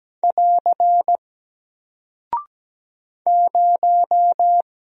Morse code, heard as a keyed tone, about 17 wpm, 700 Hz. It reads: <AR> 0
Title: Single Character (Letter:Number:Punctuation:Prosign) 17wpm